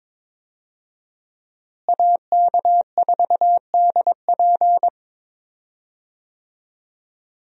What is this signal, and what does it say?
Morse code, heard as a keyed tone, about 21 wpm, 700 Hz. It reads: AK4DP
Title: USA Call Signs - 5 Characters - Repeated 3x - 22wpm